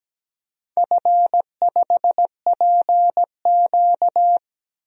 Morse code, heard as a keyed tone, about 17 wpm, 700 Hz. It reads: F5PQ
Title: DXCC-100 Call Signs - 4 Characters - 17wpm